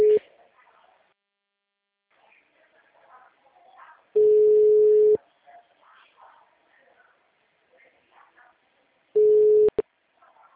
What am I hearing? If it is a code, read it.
Ringback tone (Europe, 425 Hz), with speech.